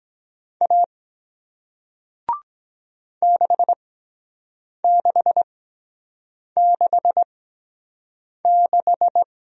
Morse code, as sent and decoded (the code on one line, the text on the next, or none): .- / -.... -.... -.... -....
A 6666